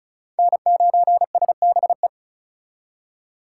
Morse code, heard as a keyed tone, about 48 wpm, spaced as for 42 wpm, 700 Hz. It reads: N9SBE
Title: Speed Racing - USA 1x3 Call Signs 35wpm